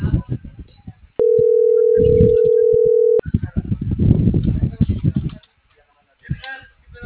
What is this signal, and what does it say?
Ringback tone (North America), 440 + 480 Hz.